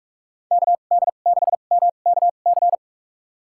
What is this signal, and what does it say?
Morse code, heard as a keyed tone, about 46 wpm, spaced as for 33 wpm, 700 Hz. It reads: KD6MKC
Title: USA Call Signs - 6 Characters - 2x Character Spacing - 45wpm